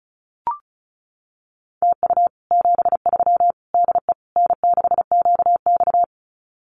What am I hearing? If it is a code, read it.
Morse code, heard as a keyed tone, about 34 wpm, 700 Hz. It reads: TU 73 DE N6QX
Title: Common QSO Phrases Encore 35wpm